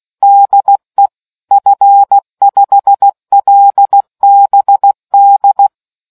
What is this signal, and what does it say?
Morse code, heard as a keyed tone, about 16 wpm, 800 Hz. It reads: DE F5LBD